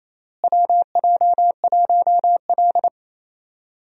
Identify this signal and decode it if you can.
Morse code, heard as a keyed tone, about 27 wpm, 700 Hz. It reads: WJ1L
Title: USA Call Signs - 4 Characters - Repeated 2x - 28wpm